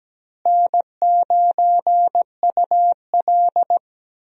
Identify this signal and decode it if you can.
Morse code, heard as a keyed tone, about 17 wpm, 700 Hz. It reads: N9UL
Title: USA Call Signs - 4 Characters - Repeated 3x - 17wpm